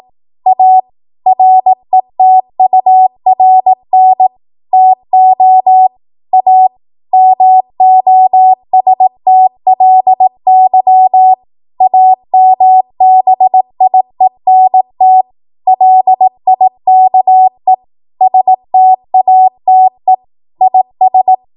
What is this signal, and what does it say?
Morse code, heard as a keyed tone, about 18 wpm, 750 Hz. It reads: A RETURN TO A MOSTLY AMBIENT LIKE STATE IS
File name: Morse / 18WPM